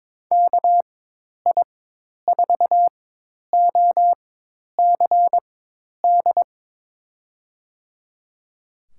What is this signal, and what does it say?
Morse code, heard as a keyed tone, about 22 wpm, spaced as for 10 wpm, 700 Hz. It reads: KI4OCD
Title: USA Call Signs - 6 Characters - 4x Character Spacing - 22wpm